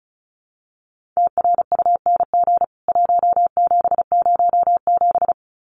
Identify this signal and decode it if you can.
Morse code, heard as a keyed tone, about 34 wpm, 700 Hz. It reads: TRUNG 1707
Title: CWT Exchange 35wpm